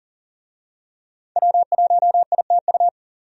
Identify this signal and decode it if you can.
Morse code, heard as a keyed tone, about 40 wpm, 700 Hz. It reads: W1ITU
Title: USA Call Signs - 5 Characters - 40wpm